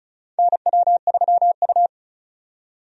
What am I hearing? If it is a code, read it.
Morse code, heard as a keyed tone, about 36 wpm, 700 Hz. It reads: NW3U
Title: USA Call Signs - 4 Characters - 35wpm